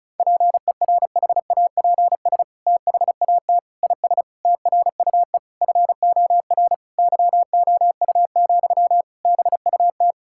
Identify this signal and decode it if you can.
Morse code, heard as a keyed tone, about 34 wpm, 700 Hz. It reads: PERHAPS THAT IS TRUE FOR YOU, BUT
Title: Sentences from Top 1000 Words - Review Part 5 - 35wpm